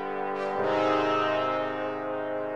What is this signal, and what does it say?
There is no code.